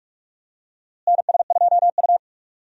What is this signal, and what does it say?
Morse code, heard as a keyed tone, about 46 wpm, 700 Hz. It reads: NS2U
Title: USA Call Signs - 4 Characters - Repeated 3x - 45wpm